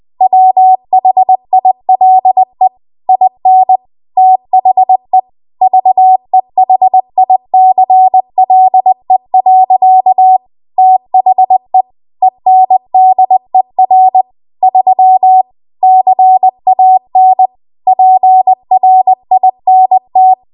Morse code, heard as a keyed tone, about 20 wpm, 750 Hz. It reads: WHILE IN THE VEHICLE. THE ENDER 3 CAN PRINT
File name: Archive / 20 WPM / 20WPM